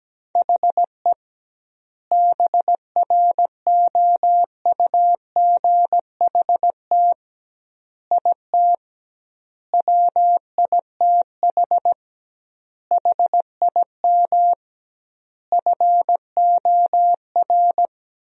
Morse code, heard as a keyed tone, about 17 wpm, 700 Hz. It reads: HE BROUGHT IT WITH HIM FOR
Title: 2x Word-Spacing - Sentences from Top 500 Words - 17wpm